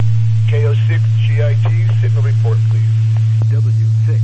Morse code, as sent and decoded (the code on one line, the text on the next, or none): none